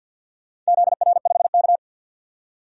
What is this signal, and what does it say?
Morse code, heard as a keyed tone, about 52 wpm, 700 Hz. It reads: ZR5K
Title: DXCC-100 Call Signs - 4 Characters - 50wpm